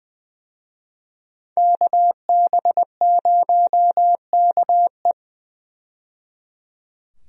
Morse code, heard as a keyed tone, about 20 wpm, 700 Hz. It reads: KB0KE
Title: USA Call Signs - 5 Characters - 20wpm